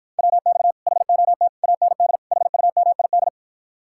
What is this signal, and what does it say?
Morse code, heard as a keyed tone, about 54 wpm, 700 Hz. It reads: WX HOT AND HUMID